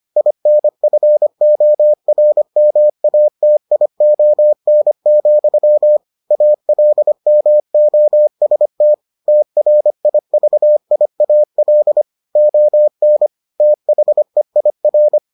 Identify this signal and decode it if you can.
Morse code, heard as a keyed tone, about 25 wpm, 600 Hz. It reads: INFORMATION, ALMOST TRIVIAL ON THEIR